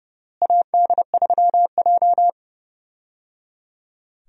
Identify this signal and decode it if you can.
Morse code, heard as a keyed tone, about 30 wpm, 700 Hz. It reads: AD3J
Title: USA Call Signs - 4 Characters - Repeated 3x - 30wpm